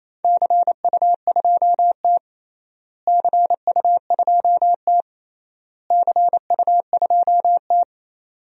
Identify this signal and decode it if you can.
Morse code, heard as a keyed tone, about 28 wpm, 700 Hz. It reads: CU2T CU2T CU2T
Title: DXCC-100 Call Signs - 4 Characters - Repeated 3x - 28wpm